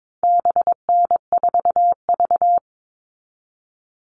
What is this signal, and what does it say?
Morse code, heard as a keyed tone, about 22 wpm, 700 Hz. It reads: BN4V
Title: All Cut Numbers - 4-Digit 22wpm